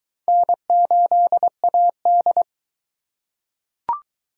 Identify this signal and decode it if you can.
Morse code, heard as a keyed tone, about 20 wpm, 700 Hz. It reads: N8AD